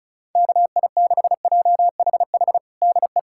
Morse code, heard as a keyed tone, about 36 wpm, 700 Hz. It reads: KI6JHH DE
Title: USA Callsign DE Callsign 35wpm